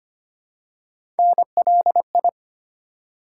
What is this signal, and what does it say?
Morse code, heard as a keyed tone, about 25 wpm, 700 Hz. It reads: NLI
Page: morse-code-ninja-practice-25wpm